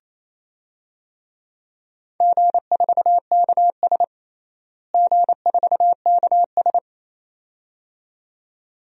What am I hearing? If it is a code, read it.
Morse code, heard as a keyed tone, about 28 wpm, 700 Hz. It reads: G4KS G4KS
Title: DXCC-100 Call Signs - 4 Characters - Repeated 2x - 28wpm